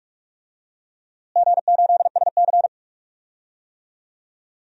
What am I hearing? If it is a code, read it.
Morse code, heard as a keyed tone, about 41 wpm, 700 Hz. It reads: G8SC
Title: DXCC-100 Call Signs - 4 Characters - Repeated 3x - 45wpm